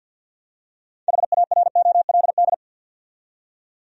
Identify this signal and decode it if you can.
Morse code, heard as a keyed tone, about 48 wpm, 700 Hz. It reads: HAROLD
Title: USA Common Names - 50wpm